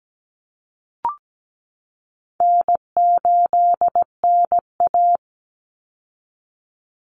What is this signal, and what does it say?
Morse code, heard as a keyed tone, about 17 wpm, 700 Hz. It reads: N8NA
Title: Easy Call Signs Repeated 3x - 17wpm